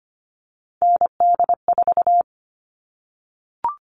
Morse code, heard as a keyed tone, about 25 wpm, 700 Hz. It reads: ND4
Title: All Cut Numbers - 3-Digit 25wpm